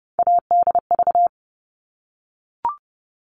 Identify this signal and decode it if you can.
Morse code, heard as a keyed tone, about 30 wpm, 700 Hz. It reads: ADV
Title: All Cut Numbers - 3-Digit 30wpm